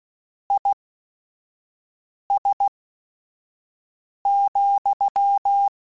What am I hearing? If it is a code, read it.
Morse code, heard as a keyed tone, about 16 wpm, spaced as for 5 wpm, 800 Hz. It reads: IS,